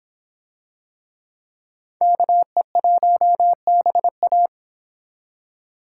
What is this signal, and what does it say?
Morse code, heard as a keyed tone, about 26 wpm, 700 Hz. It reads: KE1BA